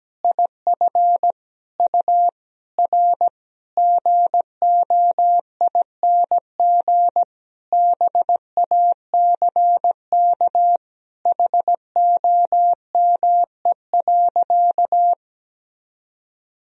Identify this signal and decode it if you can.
Morse code, heard as a keyed tone, about 17 wpm, 700 Hz. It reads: IF U R GOING BACK HOME.
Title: Sentences from Top 500 Words and Abbreviations 17wpm